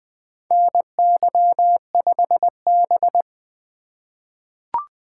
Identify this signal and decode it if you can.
Morse code, heard as a keyed tone, about 20 wpm, 700 Hz. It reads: NY5B
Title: USA Call Signs - 4 Characters - 2x Character Spacing - 20wpm